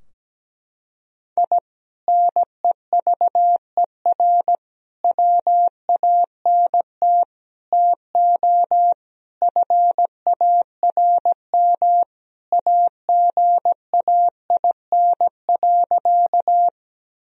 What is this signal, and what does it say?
Morse code, heard as a keyed tone, about 17 wpm, 700 Hz. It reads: I NEVER WANT TO FARM AGAIN.
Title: Sentences from Top 1000 Words - Review Part 2 - 17wpm